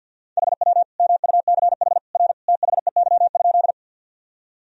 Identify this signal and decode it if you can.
Morse code, heard as a keyed tone, about 46 wpm, 700 Hz. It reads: HW MUCH R THEY?
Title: Sentences from Top 200 Words and Abbreviations 50wpm